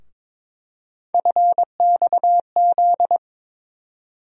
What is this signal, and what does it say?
Morse code, heard as a keyed tone, about 22 wpm, 700 Hz. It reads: FXZ
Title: USA Call Sign Postfixes - 3 Letters 22wpm